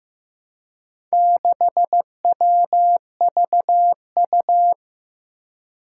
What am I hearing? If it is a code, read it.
Morse code, heard as a keyed tone, about 15 wpm, 700 Hz. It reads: 6WVU
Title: Four Challenging Letter-Number Combinations 15wpm